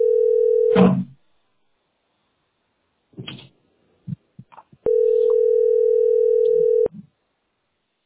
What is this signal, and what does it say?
Ringback tone (North America), 440 + 480 Hz.